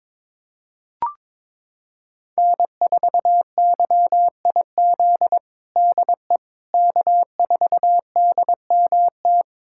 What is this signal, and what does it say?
Morse code, heard as a keyed tone, about 22 wpm, 700 Hz. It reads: N4YIZ DE K4DMT
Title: USA Callsign DE Callsign - 5 Characters - 22wpm